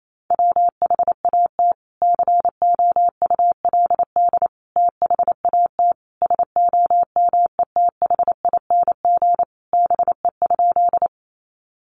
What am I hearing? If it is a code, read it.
Morse code, heard as a keyed tone, about 28 wpm, 700 Hz. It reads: WHAT COULD THAT SOMETHING BE?
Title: Sentences from Top 600 Words 28wpm